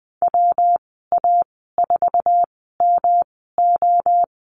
Morse code, heard as a keyed tone, about 20 wpm, spaced as for 14 wpm, 700 Hz. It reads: WA4MO
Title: 186 Lesson 30g - Letter b - Call Signs 20wpm